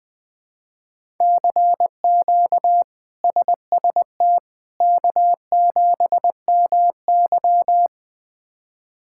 Morse code, heard as a keyed tone, about 20 wpm, 700 Hz. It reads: CQ SST K7MY
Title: K1USN SST QSO Practice - 20wpm